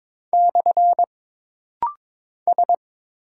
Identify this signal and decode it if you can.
Morse code, heard as a keyed tone, about 22 wpm, 700 Hz. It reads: / S
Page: This rapid-fire timing is geared to push your ICR (Instant Character Recognition) skills.